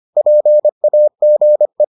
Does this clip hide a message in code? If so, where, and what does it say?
Yes: Morse code from 0.2 to 1.8 s, heard as a keyed tone, about 25 wpm, 600 Hz. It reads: PAGE